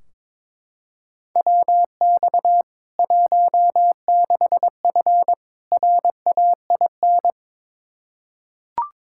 Morse code, heard as a keyed tone, about 22 wpm, 700 Hz. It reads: WX 16F RAIN
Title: K1USN SST QSO Practice 22wpm